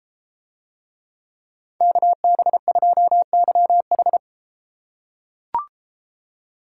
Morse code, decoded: KB2YH

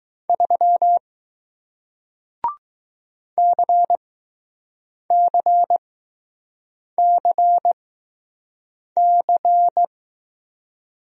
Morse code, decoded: 3 CCCC

